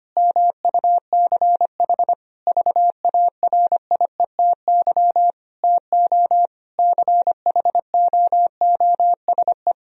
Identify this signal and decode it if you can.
Morse code, heard as a keyed tone, about 25 wpm, 700 Hz. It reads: MUCH VARIETY TO CHOOSE
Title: Sentences from Top 1000 Words - Review Part 4 - 25wpm